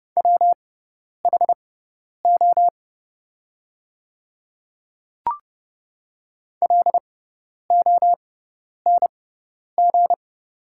Morse code, heard as a keyed tone, about 31 wpm, spaced as for 10 wpm, 700 Hz. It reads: WHO LONG
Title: 6x Character Spacing - Top 200 Words - 30wpm